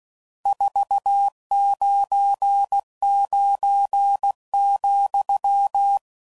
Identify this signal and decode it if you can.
Morse code, heard as a keyed tone, about 16 wpm, 800 Hz. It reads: 499,
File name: CW Code Groups @15WPM / RSTs / Various RST Nos